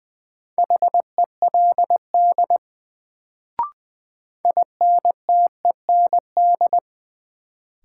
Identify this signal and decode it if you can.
Morse code, heard as a keyed tone, about 20 wpm, 700 Hz. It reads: HELD INTEND
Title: Sentences from Top 1600 Words - New Word Drill - 20wpm